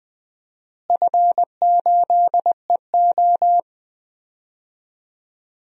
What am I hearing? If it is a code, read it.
Morse code, heard as a keyed tone, about 20 wpm, 700 Hz. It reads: F8EO
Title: DXCC-100 Call Signs - 4 Characters - Repeated 3x - 20wpm